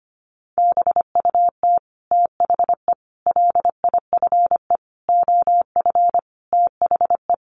Morse code, heard as a keyed tone, about 25 wpm, 700 Hz. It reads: BUT THE LIFE OF THE
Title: Sentences from Top 1500 Words 25wpm